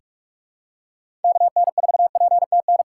Morse code, heard as a keyed tone, about 47 wpm, 700 Hz. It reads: KN4PTN
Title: USA Call Signs - 4x Character Spacing - 45wpm